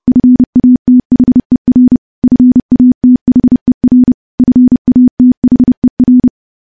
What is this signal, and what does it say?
Morse code, heard as a keyed tone, about 32 wpm, 250 Hz. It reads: FATHER FATHER FATHER
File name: morse / morse-training-30WPM